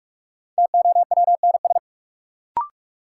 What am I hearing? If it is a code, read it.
Morse code, heard as a keyed tone, about 46 wpm, 700 Hz. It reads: TOWNS
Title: Sentences from Top 1000 Words - Review Part 6 - New Word Drill - 45wpm